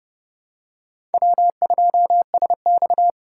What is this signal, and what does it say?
Morse code, heard as a keyed tone, about 30 wpm, 700 Hz. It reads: W2SX
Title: USA Call Signs - 4 Characters - 2x Character Spacing - 30wpm